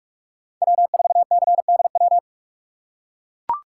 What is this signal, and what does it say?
Morse code, heard as a keyed tone, about 42 wpm, 690 Hz. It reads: W4CDW
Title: USA Call Signs - 5 Characters - 2x Character Spacing - 45wpm